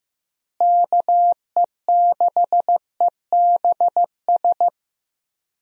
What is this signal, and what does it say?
Morse code, heard as a keyed tone, about 15 wpm, 700 Hz. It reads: KE6EBS
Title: USA Call Signs - 6 Characters - Repeated 3x - 15wpm